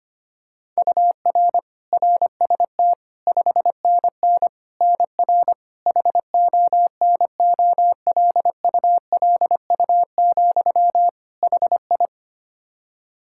Morse code, heard as a keyed tone, about 25 wpm, 700 Hz. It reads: UR RST 5NN NR HONOLULU, HI
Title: Practice QSO - Exchange 1 - Beginning - 25wpm